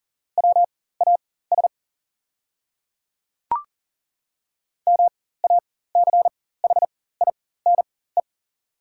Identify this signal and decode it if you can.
Morse code, heard as a keyed tone, about 40 wpm, spaced as for 19 wpm, 690 Hz. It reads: WAS MACHINE
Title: 4x Character Spacing - Top 500 Words - 40wpm